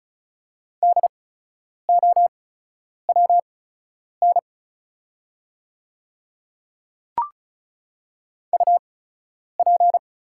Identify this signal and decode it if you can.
Morse code, heard as a keyed tone, about 36 wpm, spaced as for 10 wpm, 700 Hz. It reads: DOWN UP
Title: 8x Character Spacing - Top 100 Words - 35wpm